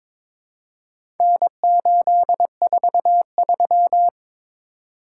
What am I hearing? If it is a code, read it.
Morse code, heard as a keyed tone, about 22 wpm, 690 Hz. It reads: N843
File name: practice / 22wpm / Four Letter-Number Combinations - 22wpm